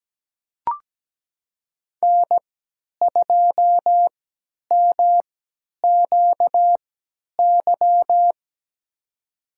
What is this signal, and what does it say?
Morse code, heard as a keyed tone, about 17 wpm, spaced as for 10 wpm, 700 Hz. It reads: N2MQY
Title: USA Call Signs - 5 Characters - 3x Character Spacing - 17wpm